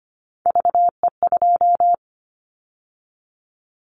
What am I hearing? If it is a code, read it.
Morse code, heard as a keyed tone, about 25 wpm, 700 Hz. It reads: VE2
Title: Callsign Prefixes for Top 50 DXCC - 25wpm